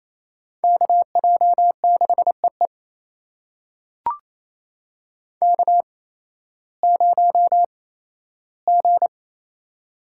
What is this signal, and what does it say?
Morse code, heard as a keyed tone, about 28 wpm, spaced as for 15 wpm, 700 Hz. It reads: KJ6EE K0G
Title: USA Call Signs - 5 Characters - 8x Character Spacing - 28wpm